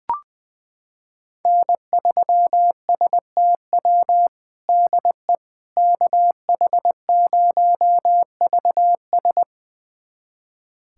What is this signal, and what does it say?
Morse code, heard as a keyed tone, about 21 wpm, 700 Hz. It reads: N3STW DE KH0VS
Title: USA Callsign DE Callsign - 5 Characters - 20wpm